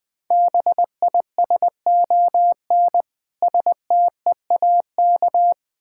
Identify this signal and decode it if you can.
Morse code, heard as a keyed tone, about 20 wpm, 700 Hz. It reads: BISON STEAK
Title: Difficult Words with Hints - 20wpm